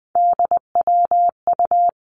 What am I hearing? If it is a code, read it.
Morse code, heard as a keyed tone, about 19 wpm, 700 Hz. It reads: DWU
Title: Three Challenging Letter-Number Combinations 20wpm